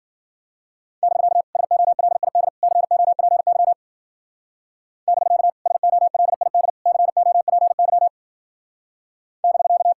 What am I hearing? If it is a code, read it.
Morse code, heard as a keyed tone, about 49 wpm, 700 Hz. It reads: <BK> SOLID COPY <BK> SOLID COPY <BK>
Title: Speed-Racing Common QSO Phrases Encore - 40wpm